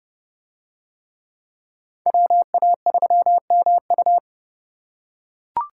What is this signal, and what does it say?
Morse code, heard as a keyed tone, about 27 wpm, 700 Hz. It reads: WA3MU